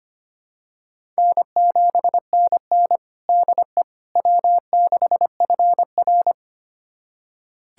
Morse code, heard as a keyed tone, about 25 wpm, 700 Hz. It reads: N7NN DE W6FR
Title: USA Callsign DE Callsign - 4 Characters - 25wpm